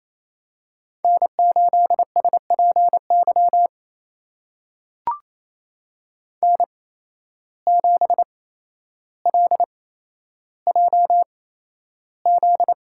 Morse code, heard as a keyed tone, about 28 wpm, spaced as for 12 wpm, 700 Hz. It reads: N8SPY N7LJZ
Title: USA Call Signs - 5 Characters - 8x Character Spacing - 28wpm